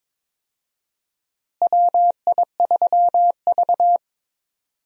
Morse code, heard as a keyed tone, about 22 wpm, 700 Hz. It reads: WI3V
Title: USA Call Signs - 4 Characters - Repeated 2x - 22wpm